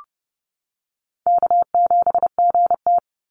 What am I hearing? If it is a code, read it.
Morse code, heard as a keyed tone, about 31 wpm, 700 Hz. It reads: K7GT
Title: NA SOTA Chaser Callsigns 30wpm